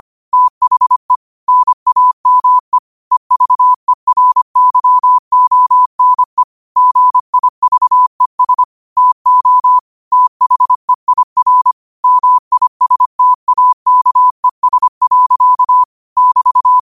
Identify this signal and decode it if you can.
Morse code, heard as a keyed tone, about 25 wpm, 1000 Hz. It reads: THE NAME EVERYONE GIVES TO THEIR MISTAKES. <BT>
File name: qotdfiles / 25wpm